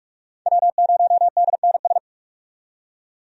Morse code, heard as a keyed tone, about 44 wpm, 700 Hz. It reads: W0DNS
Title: USA Call Signs - 5 Characters - 45wpm